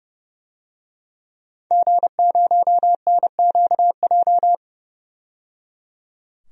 Morse code, heard as a keyed tone, about 30 wpm, 700 Hz. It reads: G0NQJ
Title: DXCC-100 Call Signs - 5 Characters - 30wpm